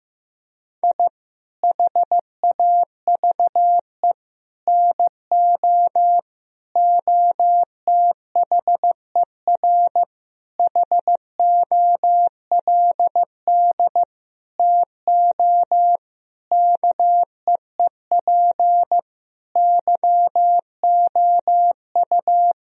Morse code, heard as a keyed tone, about 16 wpm, 700 Hz. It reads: I HAVE NO OTHER HOLD TO KEEP YOU
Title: Sentences from Top 1000 Words - Review Part 2 - 15wpm